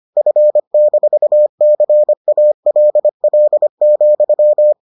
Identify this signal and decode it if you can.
Morse code, heard as a keyed tone, about 25 wpm, 600 Hz. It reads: F-CALL,